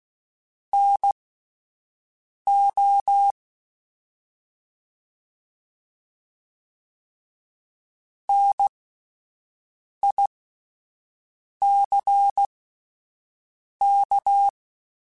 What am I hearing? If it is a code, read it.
Morse code, heard as a keyed tone, about 16 wpm, spaced as for 5 wpm, 800 Hz. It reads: NO NICK